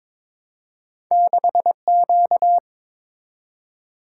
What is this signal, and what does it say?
Morse code, heard as a keyed tone, about 22 wpm, 690 Hz. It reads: 6Q
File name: practice / 22wpm / Two Letter-Number Combinations - 22wpm